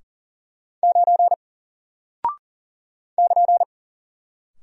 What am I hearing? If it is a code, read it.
Morse code, heard as a keyed tone, about 40 wpm, 700 Hz. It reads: 9 <KN>
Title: Single Character (Letter:Number:Punctuation:Prosign) Alt 40wpm